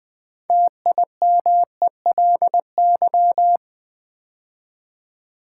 Morse code, heard as a keyed tone, about 20 wpm, 700 Hz. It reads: TIMELY
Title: Build A Word - 3 Letters At A Time - 6-Letter Words - 20wpm